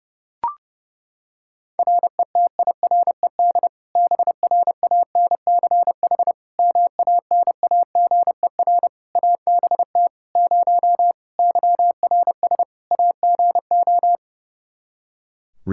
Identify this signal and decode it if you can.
Morse code, heard as a keyed tone, about 30 wpm, 700 Hz. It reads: RETIRED BRANCH MANAGER ABT 0 YRS AGO
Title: Practice QSO - Exchange 3 - More Info - 30wpm